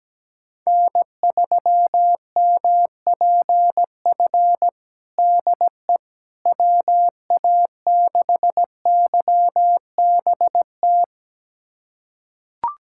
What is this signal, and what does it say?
Morse code, heard as a keyed tone, about 17 wpm, 700 Hz. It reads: N3MPF DE WA6YBT
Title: USA Callsign DE Callsign 17wpm